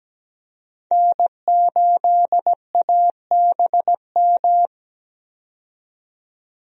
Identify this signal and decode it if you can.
Morse code, heard as a keyed tone, about 17 wpm, 700 Hz. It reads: N8ABM
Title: USA Call Signs - 5 Characters - 17wpm